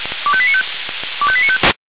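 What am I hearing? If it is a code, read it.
Five-tone selective call (ZVEI), heard as quick runs of five tones, tones of about 70 ms. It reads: 25895 25895
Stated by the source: ZVEI five-tone selective call 25431, received with an SDR.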